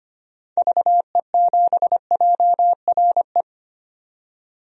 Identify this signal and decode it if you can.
Morse code, heard as a keyed tone, about 25 wpm, 700 Hz. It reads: VE7JRE
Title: DXCC-100 Call Signs - 6 Characters - 25wpm